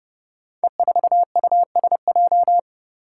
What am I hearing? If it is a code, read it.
Morse code, heard as a keyed tone, about 29 wpm, 700 Hz. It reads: E4USJ
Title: Five Challenging Letter-Number Combinations 30wpm